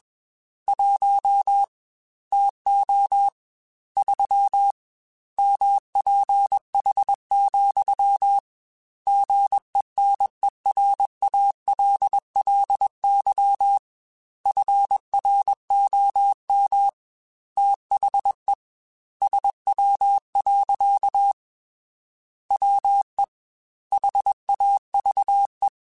Morse code, heard as a keyed tone, about 21 wpm, 800 Hz. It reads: 1 TO 3 MPH, GENERALLY FROM THE SW. WE HAVE